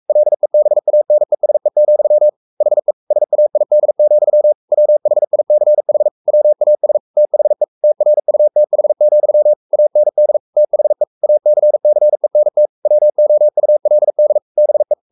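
Morse code, heard as a keyed tone, about 44 wpm, 600 Hz. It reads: LEBANESE, HE SAID, WHICH WAS THE TRUTH, AND THE ACCENT WOULD BE
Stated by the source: Morse code at 43 wpm sending Ken Follett's novel Never in English, from CW med Gnister.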